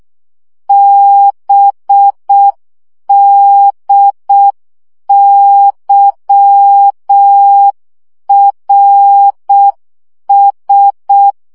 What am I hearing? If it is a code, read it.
Morse code, heard as a keyed tone, about 6 wpm, 800 Hz. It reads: BDYRS
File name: morze vizsga 2011-3 5